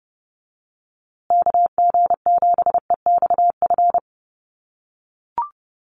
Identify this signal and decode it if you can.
Morse code, heard as a keyed tone, about 30 wpm, 700 Hz. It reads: KG7EXF
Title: USA Call Signs Repeated 3x 30wpm